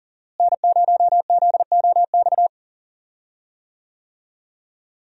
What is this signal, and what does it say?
Morse code, heard as a keyed tone, about 37 wpm, 700 Hz. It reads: N0ZOX
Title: USA Call Signs - 5 Characters - Repeated 2x - 40wpm